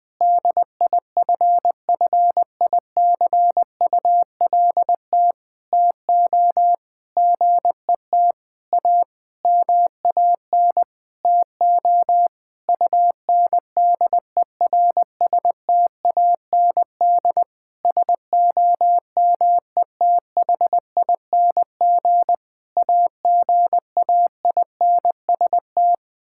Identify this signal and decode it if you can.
Morse code, heard as a keyed tone, about 21 wpm, 700 Hz. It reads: DIFFICULT TO GET A MAN TO UNDERSTAND SOMETHING AGAINST
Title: Sentences with 9-Word Phrases - 20wpm